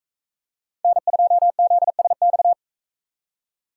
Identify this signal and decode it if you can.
Morse code, heard as a keyed tone, about 42 wpm, 700 Hz. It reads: N2ZSX